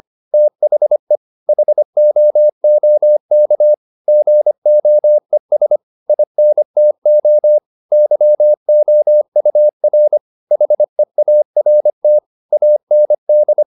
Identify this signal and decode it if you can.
Morse code, heard as a keyed tone, about 25 wpm, 600 Hz. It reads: THE HOOK GOES INTO YOUR HEART AND
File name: the old man and the sea part 1 25wpm